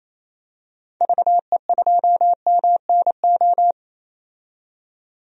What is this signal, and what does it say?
Morse code, heard as a keyed tone, about 28 wpm, 700 Hz. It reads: VE2MNO
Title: DXCC-100 Call Signs - 6 Characters - 28wpm